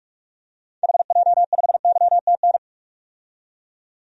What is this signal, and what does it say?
Morse code, heard as a keyed tone, about 46 wpm, 700 Hz. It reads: HJ5YTN
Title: DXCC-100 Call Signs - 6 Characters - 45wpm